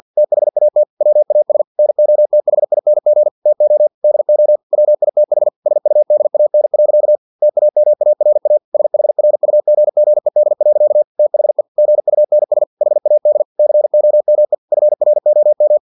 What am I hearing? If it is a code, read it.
Morse code, heard as a keyed tone, about 50 wpm, 600 Hz. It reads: THAT WAS NOTHING TO DO WITH SUDAN. TAMARA SHRUGGED. THE GUNS HAD COME FROM
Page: Never 49wpm